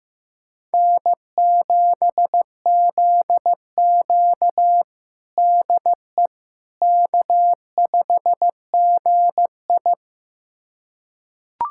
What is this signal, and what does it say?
Morse code, heard as a keyed tone, about 15 wpm, 700 Hz. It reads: N7ZQ DE K5GI
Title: USA Callsign DE Callsign - 4 Characters - 15wpm